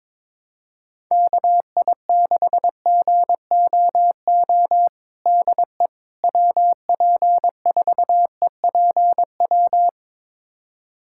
Morse code, heard as a keyed tone, about 22 wpm, 700 Hz. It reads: KI6GOO DE WP4EPW
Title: USA Callsign DE Callsign - 6 Characters - 22wpm